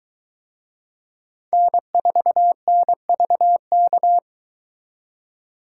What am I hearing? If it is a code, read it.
Morse code, heard as a keyed tone, about 23 wpm, 700 Hz. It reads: N4NVK